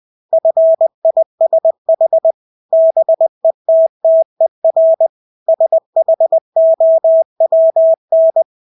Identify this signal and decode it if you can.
Morse code, heard as a keyed tone, about 20 wpm, 650 Hz. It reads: FISH BETTER SHOWN
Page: Engelske ord 20 wpm